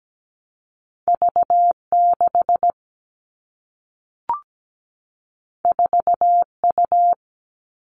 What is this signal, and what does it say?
Morse code, heard as a keyed tone, about 17 wpm, 700 Hz. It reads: V6 4U
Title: All Cut Numbers - 2-Digit 17wpm